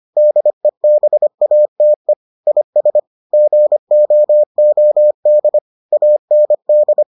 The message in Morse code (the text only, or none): DEBATE IS GOOD AND